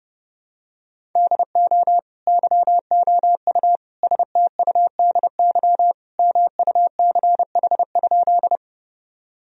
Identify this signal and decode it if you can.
Morse code, heard as a keyed tone, about 32 wpm, 700 Hz. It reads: DO YOU STUDY MUCH?
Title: Sentences from Top 1000 Words - Review Part 1 - 30wpm